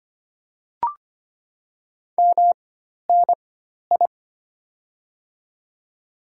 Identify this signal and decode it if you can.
Morse code, heard as a keyed tone, about 25 wpm, spaced as for 12 wpm, 700 Hz. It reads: MNI
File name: Top 90 QSO Elements - 4x Character Spacing - 25wpm